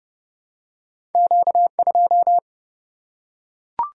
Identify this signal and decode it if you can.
Morse code, heard as a keyed tone, about 28 wpm, 700 Hz. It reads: Q 2
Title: Single-Letter Single-Number 30wpm